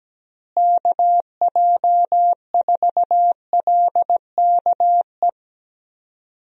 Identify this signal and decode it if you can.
Morse code, heard as a keyed tone, about 17 wpm, 700 Hz. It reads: KJ4LKE
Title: USA Call Signs - 6 Characters - Repeated 2x - 17wpm